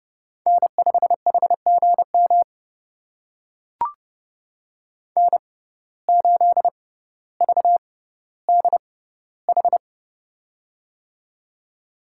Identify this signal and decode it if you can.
Morse code, heard as a keyed tone, about 32 wpm, spaced as for 16 wpm, 700 Hz. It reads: N5HGM N8VDH
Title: USA Call Signs - 5 Characters - 6x Character Spacing - 30wpm